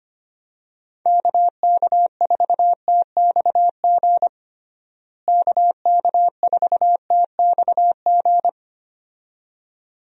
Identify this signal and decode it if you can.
Morse code, heard as a keyed tone, about 25 wpm, 700 Hz. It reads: KK4TXG KK4TXG
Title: USA Call Signs - 6 Characters - Repeated 2x - 25wpm